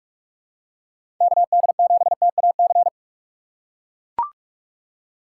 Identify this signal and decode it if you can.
Morse code, heard as a keyed tone, about 41 wpm, 700 Hz. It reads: KD7TAC
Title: USA Call Signs - 6 Characters - 3x Character Spacing - 45wpm